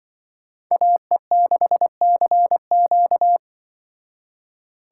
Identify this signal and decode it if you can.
Morse code, heard as a keyed tone, about 25 wpm, 700 Hz. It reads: AE6CQ
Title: Speed Racing - USA 2x2 Call Signs 20wpm